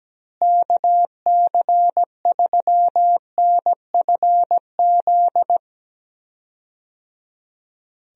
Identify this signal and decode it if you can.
Morse code, heard as a keyed tone, about 17 wpm, 700 Hz. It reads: KC3NFZ
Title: USA Call Signs - 6 Characters - Repeated 3x - 17wpm